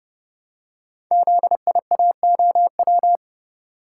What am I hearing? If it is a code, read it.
Morse code, heard as a keyed tone, about 30 wpm, 700 Hz. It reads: ZIAOW
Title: Code Groups - 5 Letter:Numbers - 30wpm